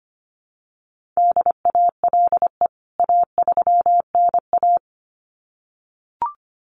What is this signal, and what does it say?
Morse code, heard as a keyed tone, about 25 wpm, 700 Hz. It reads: DALE A3NA
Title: CWT Exchange 25wpm